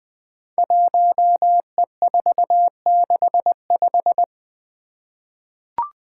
Morse code, heard as a keyed tone, about 20 wpm, 700 Hz. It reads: 1E465
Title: Code Groups - 5 Letter:Numbers - 20wpm